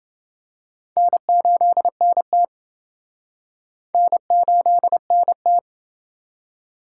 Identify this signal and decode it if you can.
Morse code, heard as a keyed tone, about 27 wpm, 700 Hz. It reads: N8NT N8NT